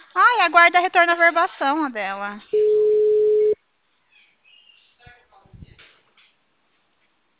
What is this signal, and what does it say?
Ringback tone (Europe, 425 Hz), with speech.
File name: out- -2004- - - .52480